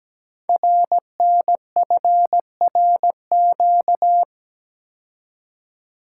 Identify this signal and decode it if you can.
Morse code, heard as a keyed tone, about 17 wpm, 700 Hz. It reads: RNFRQ
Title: Code Groups - 5 Letters - 17wpm